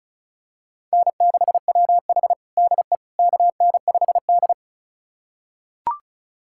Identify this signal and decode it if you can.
Morse code, heard as a keyed tone, about 36 wpm, 700 Hz. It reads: N6WH DE KN5D
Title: USA Callsign DE Callsign - 4 Characters - 35wpm